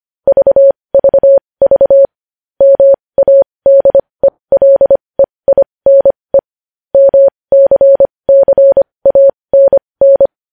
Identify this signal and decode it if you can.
Morse code, heard as a keyed tone, about 25 wpm, spaced as for 20 wpm, 550 Hz. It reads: VVV MADELEINE MCCANN
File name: podcast / News headlines-1x-25x20